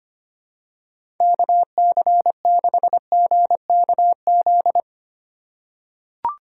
Morse code, heard as a keyed tone, about 25 wpm, 700 Hz. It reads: KC6GKZ
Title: USA Call Signs - 4x Character Spacing - 25wpm